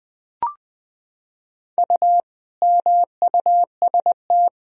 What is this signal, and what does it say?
Morse code, heard as a keyed tone, about 21 wpm, 700 Hz. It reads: U MUST
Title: Sentences from Top 400 Words and Abbreviations 20wpm